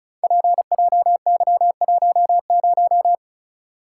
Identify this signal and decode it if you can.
Morse code, heard as a keyed tone, about 36 wpm, 700 Hz. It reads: PJY10
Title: Code Groups - 5 Letter:Numbers - 35wpm